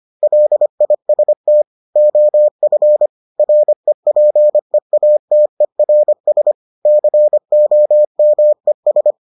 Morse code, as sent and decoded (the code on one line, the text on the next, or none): .-.. .. ... - / --- ..-. / .-. .--. .- - . .-. ... / -.-. --- -- . ...
LIST OF REPEATERS COMES